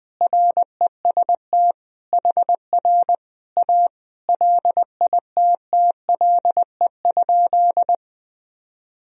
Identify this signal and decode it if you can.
Morse code, heard as a keyed tone, about 21 wpm, 700 Hz. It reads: REST HR A LITTLE?
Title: Sentences from Top 500 Words and Abbreviations 20wpm